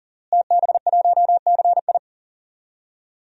Morse code, heard as a keyed tone, about 37 wpm, 700 Hz. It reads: TB1CI
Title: DXCC-100 Call Signs - 5 Characters - Repeated 3x - 40wpm